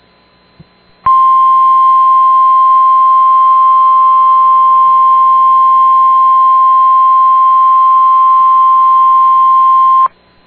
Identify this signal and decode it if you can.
Emergency Alert System (SAME) tones (the 1050 Hz attention tone). It is heard from 1.0 to 10.1 s.